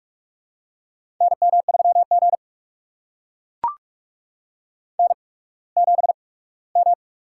Morse code, heard as a keyed tone, about 44 wpm, spaced as for 22 wpm, 700 Hz. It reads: NM3G N7M
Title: USA Call Signs - 4 Characters - 8x Character Spacing - 45wpm